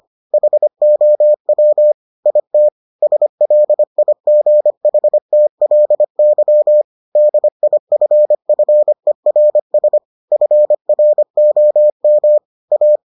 Morse code, heard as a keyed tone, about 25 wpm, 600 Hz. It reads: HOW IT SLIGHTLY DIFFERS FROM A